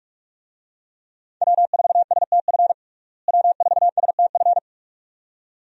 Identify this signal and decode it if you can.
Morse code, heard as a keyed tone, about 46 wpm, 700 Hz. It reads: W4STF W4STF
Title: USA Call Signs - 5 Characters - Repeated 2x - 45wpm